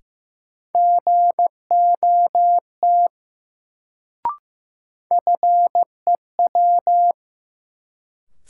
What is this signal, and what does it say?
Morse code, heard as a keyed tone, about 15 wpm, 700 Hz. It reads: GOT FEW
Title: Sentences from Top 300 Words - New Word Drill - 15wpm